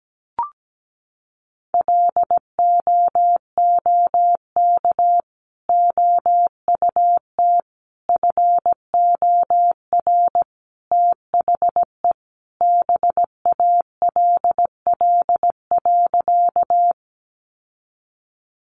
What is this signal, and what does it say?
Morse code, heard as a keyed tone, about 17 wpm, 700 Hz. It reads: LOOK OUT FOR THE BALL.
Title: Sentences from Top 900 Words 17wpm